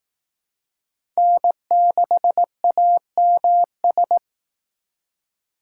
Morse code, heard as a keyed tone, about 18 wpm, 700 Hz. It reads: N6AMS
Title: Speed Racing - USA 1x3 Call Signs 15wpm